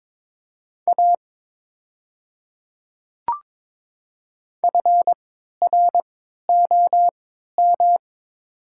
Morse code, heard as a keyed tone, about 23 wpm, spaced as for 13 wpm, 700 Hz. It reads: A FROM
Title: 3x Character Spacing - Top 100 Words - 22wpm